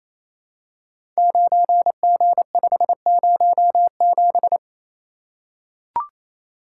Morse code, heard as a keyed tone, about 28 wpm, 700 Hz. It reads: 9G507
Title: Code Groups - 5 Letter:Numbers - 28wpm